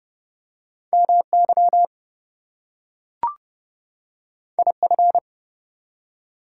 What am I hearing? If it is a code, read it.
Morse code, heard as a keyed tone, about 31 wpm, 700 Hz. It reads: MY IF
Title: Two Letter Words - 30wpm